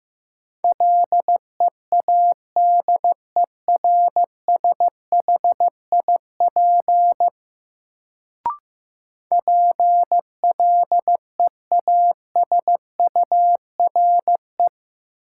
Morse code, heard as a keyed tone, about 15 wpm, 700 Hz. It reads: LEADERSHIP PLEASURE
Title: Sentences from Top 1600 Words - New Word Drill Alt - 15wpm